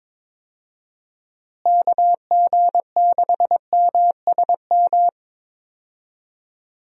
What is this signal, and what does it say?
Morse code, heard as a keyed tone, about 21 wpm, 700 Hz. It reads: KG6MSM
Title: USA Call Signs - 6 Characters - Repeated 3x - 22wpm